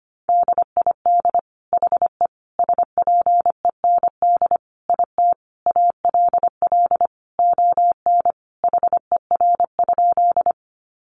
Morse code, heard as a keyed tone, about 25 wpm, 700 Hz. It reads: DID HE SPEND IT ALL ON HER?
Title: Sentences from Top 800 Words 25wpm